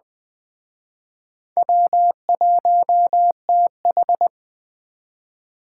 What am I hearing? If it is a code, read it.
Morse code, heard as a keyed tone, about 20 wpm, 700 Hz. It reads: W1TH